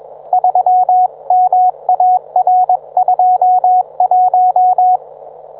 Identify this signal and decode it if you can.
Morse code, heard as a keyed tone, about 21 wpm, 730 Hz. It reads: 3MAR21
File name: book